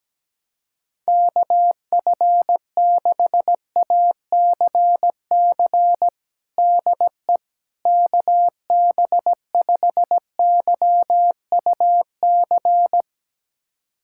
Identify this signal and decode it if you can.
Morse code, heard as a keyed tone, about 17 wpm, 700 Hz. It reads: KF6ACC DE KB5YUC
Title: USA Callsign DE Callsign 17wpm